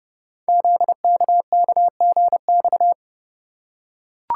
Morse code, heard as a keyed tone, about 30 wpm, 700 Hz. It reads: ZKKGX
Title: Code Groups - 5 Letter:Numbers - 30wpm